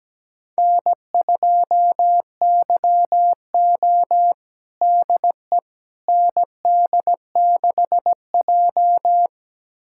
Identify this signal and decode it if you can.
Morse code, heard as a keyed tone, about 17 wpm, 700 Hz. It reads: N2YO DE ND6J
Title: USA Callsign DE Callsign - 4 Characters - 17wpm